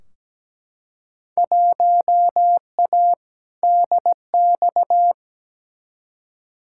Morse code, heard as a keyed tone, about 17 wpm, 700 Hz. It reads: 1A DX
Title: Field Day Exchanges Encore - 17wpm